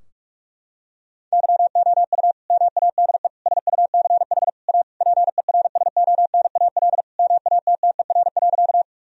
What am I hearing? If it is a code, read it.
Morse code, heard as a keyed tone, about 41 wpm, 700 Hz. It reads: YOU MADE SUCH A PERSONAL MATTER.
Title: Sentences from Top 1000 Words - Review Part 4 - 45wpm